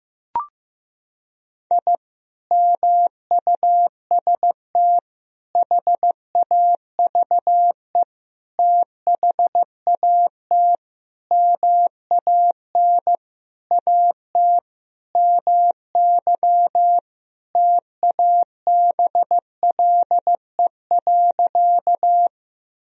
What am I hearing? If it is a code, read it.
Morse code, heard as a keyed tone, about 15 wpm, 700 Hz. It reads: I MUST HAVE THAT MAN AT MY TABLE.
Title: Sentences from Top 1000 Words - Review Part 2 - 15wpm